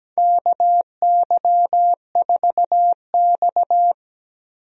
Morse code, heard as a keyed tone, about 17 wpm, 700 Hz. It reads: KY4X
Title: USA Call Signs - 4 Characters - Repeated 3x - 17wpm